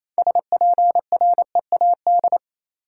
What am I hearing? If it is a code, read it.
Morse code, heard as a keyed tone, about 28 wpm, 700 Hz. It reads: SPREAD
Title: Six Letter Words - 28wpm